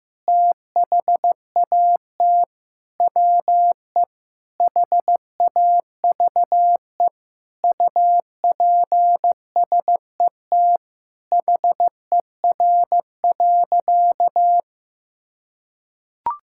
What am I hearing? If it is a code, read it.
Morse code, heard as a keyed tone, about 16 wpm, 700 Hz. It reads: THAT WE HAVE UPSET HER.
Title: Sentences from Top 1000 Words - Review Part 5 - 15wpm